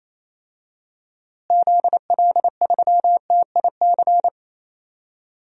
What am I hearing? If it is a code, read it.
Morse code, heard as a keyed tone, about 28 wpm, 700 Hz. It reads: ZL3TIC
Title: DXCC-100 Call Signs - 6 Characters - Repeated 2x - 28wpm